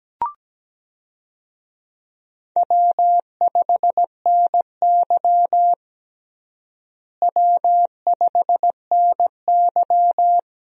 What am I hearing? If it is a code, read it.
Morse code, heard as a keyed tone, about 17 wpm, 700 Hz. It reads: W5NY W5NY
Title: USA Call Signs - 4 Characters - Repeated 3x - 17wpm